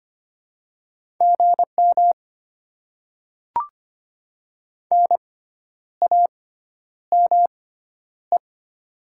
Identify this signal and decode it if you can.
Morse code, heard as a keyed tone, about 25 wpm, spaced as for 10 wpm, 700 Hz. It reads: GM NAME
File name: Top 20 QSO Elements - 6x Character Spacing - 25wpm